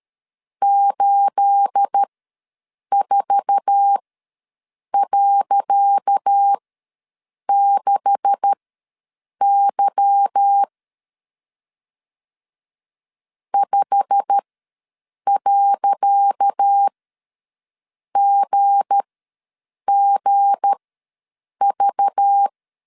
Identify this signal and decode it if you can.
Morse code, heard as a keyed tone, about 13 wpm, spaced as for 7 wpm, 800 Hz. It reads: 84.6Y 5.GGV
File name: LEZIONI / 10 SIGNS / 10C DOT ws 12wpm 3548